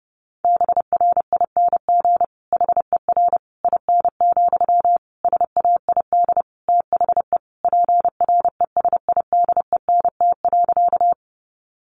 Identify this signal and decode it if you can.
Morse code, heard as a keyed tone, about 30 wpm, 700 Hz. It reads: BRING HER IN, SAID THE PRESIDENT.
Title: Sentences from Top 1000 Words 30wpm